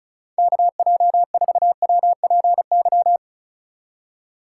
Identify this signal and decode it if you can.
Morse code, heard as a keyed tone, about 36 wpm, 700 Hz. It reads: KJ4WPY
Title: USA Call Signs - 6 Characters - 3x Character Spacing - 35wpm